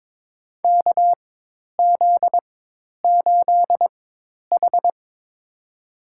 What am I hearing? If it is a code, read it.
Morse code, heard as a keyed tone, about 22 wpm, spaced as for 10 wpm, 700 Hz. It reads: KZ8H
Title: USA Call Signs - 4x Character Spacing - 22wpm